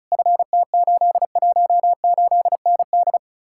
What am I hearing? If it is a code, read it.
Morse code, heard as a keyed tone, about 46 wpm, 700 Hz. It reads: FT818ND